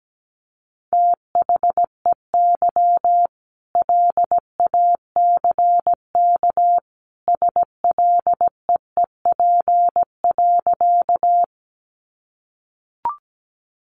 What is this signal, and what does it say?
Morse code, heard as a keyed tone, about 17 wpm, 700 Hz. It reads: THEY LACK SLEEP.
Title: Sentences from Top 1100 Words 17wpm